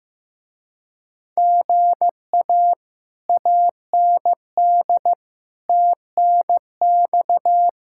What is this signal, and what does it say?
Morse code, heard as a keyed tone, about 15 wpm, 700 Hz. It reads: GA AND TNX
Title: Speed-Racing Common QSO Phrases Encore - 15wpm